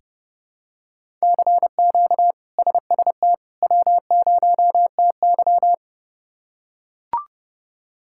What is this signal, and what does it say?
Morse code, heard as a keyed tone, about 30 wpm, 700 Hz. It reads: CQ SST W0TY
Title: K1USN SST QSO Practice - 30wpm